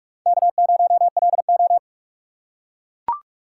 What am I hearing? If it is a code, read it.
Morse code, heard as a keyed tone, about 44 wpm, 700 Hz. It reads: K0LO
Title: USA Call Signs - 4 Characters - 6x Character Spacing - 45wpm